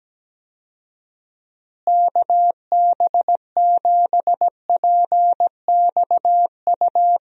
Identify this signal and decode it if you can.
Morse code, heard as a keyed tone, about 17 wpm, 700 Hz. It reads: KB7PXU
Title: USA Call Signs - 6 Characters - Repeated 3x - 17wpm